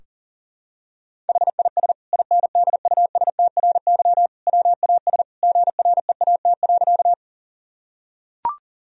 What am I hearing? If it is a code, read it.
Morse code, heard as a keyed tone, about 40 wpm, 700 Hz. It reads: HIS INDUSTRY WAS GREAT.
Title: Sentences from Top 1000 Words - Review Part 4 - 40wpm